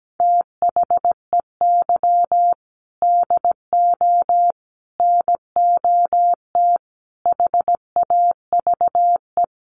Morse code, heard as a keyed tone, about 17 wpm, 700 Hz. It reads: THEY DO NOT HAVE
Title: Sets of 5 Words 17wpm